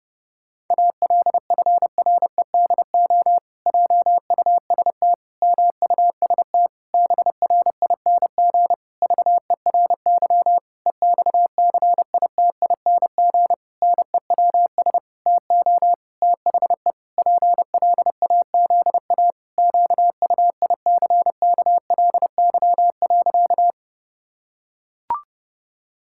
Morse code, heard as a keyed tone, about 30 wpm, 700 Hz. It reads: ALFREDO JUST MUST BRING VERY EXCITING NEWS TO THE PLAZA QUICKLY.
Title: Pangram Challenge v2 - 30wpm